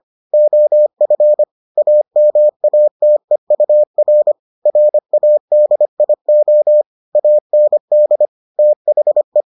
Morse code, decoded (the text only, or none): OF AMATEUR RADIO AND THE